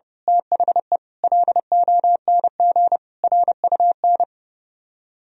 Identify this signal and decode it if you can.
Morse code, heard as a keyed tone, about 28 wpm, 700 Hz. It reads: THE LONG RUN
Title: Speed-Racing - Sets of 4 Words 30wpm